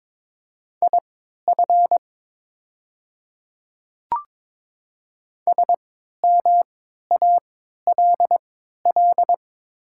Morse code, heard as a keyed tone, about 22 wpm, spaced as for 13 wpm, 690 Hz. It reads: IF SMALL
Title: 3x Character Spacing - Top 200 Words - 22wpm